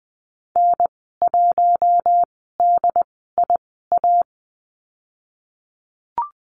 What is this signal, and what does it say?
Morse code, heard as a keyed tone, about 20 wpm, spaced as for 15 wpm, 700 Hz. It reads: N1DIA
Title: 069 Lesson 12g - Letter d - Call Signs 20wpm